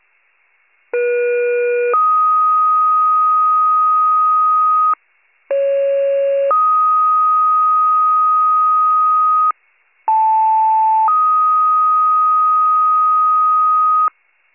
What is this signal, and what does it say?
Two-tone page, heard as a one-second tone, then a three-second tone, about 480 then 1250 Hz, about 550 then 1250 Hz, about 880 then 1250 Hz. It is heard from 0.9 to 14.1 s.